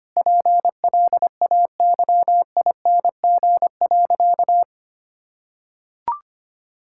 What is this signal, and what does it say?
Morse code, heard as a keyed tone, about 25 wpm, 700 Hz. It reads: PLAYING.
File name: practice / 25wpm / Difficult Sentences with Hints - 25wpm